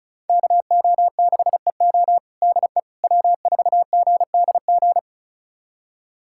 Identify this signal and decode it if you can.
Morse code, heard as a keyed tone, about 34 wpm, 700 Hz. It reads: KO6EO DE W4GDG
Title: USA Callsign DE Callsign - 5 Characters - 35wpm